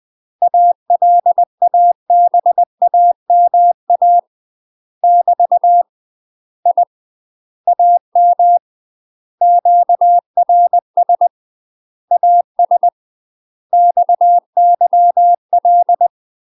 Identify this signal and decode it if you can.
Morse code, heard as a keyed tone, about 20 wpm, 700 Hz. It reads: ALABAMA <BT> I AM QRS AS XYL
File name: UK MASTER QSOs TRAINING / 20WPM / UK MASTER QSOs 0050